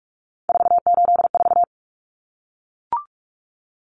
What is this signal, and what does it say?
Morse code, heard as a keyed tone, about 44 wpm, 700 Hz. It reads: 484
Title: Numbers - 3-Digit 45wpm